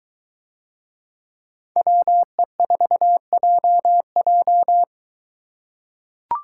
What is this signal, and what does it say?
Morse code, heard as a keyed tone, about 23 wpm, 700 Hz. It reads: WE4JJ